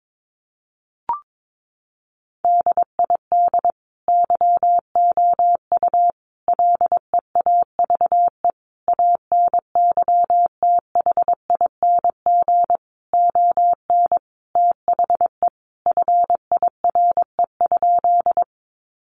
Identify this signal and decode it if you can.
Morse code, heard as a keyed tone, about 22 wpm, 700 Hz. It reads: DID YOU LEAVE ANYTHING ON THE FIRE?
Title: Sentences from Top 1000 Words 22wpm